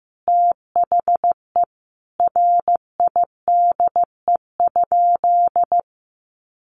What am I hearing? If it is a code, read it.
Morse code, heard as a keyed tone, about 15 wpm, 700 Hz. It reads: THE RIDE?
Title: Sentences from Top 1200 Words 15wpm